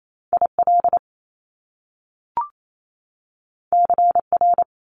Morse code, heard as a keyed tone, about 28 wpm, 700 Hz. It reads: IL CR
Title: Top Two Letter Combinations in Words 28wpm